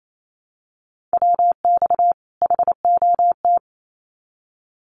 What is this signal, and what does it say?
Morse code, heard as a keyed tone, about 28 wpm, 700 Hz. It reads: WX HOT
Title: Common QSO Phrases 28wpm